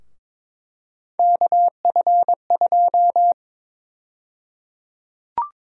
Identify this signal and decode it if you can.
Morse code, heard as a keyed tone, about 22 wpm, 700 Hz. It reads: KF2
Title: USA Call Sign Prefixes - 22wpm